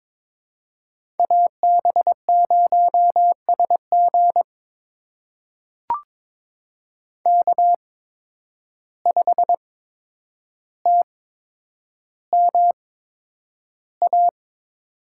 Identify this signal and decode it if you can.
Morse code, heard as a keyed tone, about 22 wpm, spaced as for 9 wpm, 700 Hz. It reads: AB0SG K5TMA
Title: USA Call Signs - 5 Characters - 8x Character Spacing - 22wpm